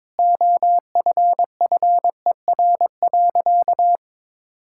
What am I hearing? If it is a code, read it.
Morse code, heard as a keyed tone, about 22 wpm, 700 Hz. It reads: OFFER.
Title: Sentences from Top 1600 Words 22wpm